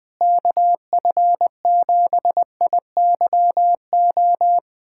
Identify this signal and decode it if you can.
Morse code, heard as a keyed tone, about 21 wpm, 700 Hz. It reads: KF7IYO
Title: USA Callsign DE Callsign - 6 Characters - 20wpm